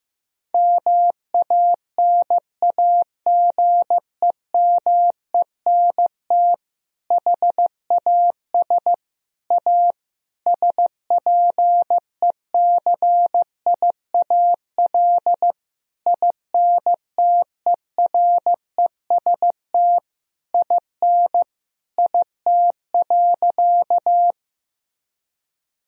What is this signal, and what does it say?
Morse code, heard as a keyed tone, about 15 wpm, 700 Hz. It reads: MANAGEMENT HAS A SPECIAL INTEREST IN IT.
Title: Sentences from Top 1000 Words - Review Part 4 - 15wpm